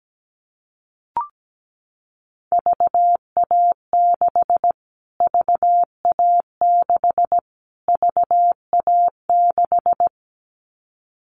Morse code, heard as a keyed tone, about 17 wpm, 700 Hz. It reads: VA6 VA6 VA6
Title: Callsign Prefixes for Top 25 DXCC - Repeated 3x - 17wpm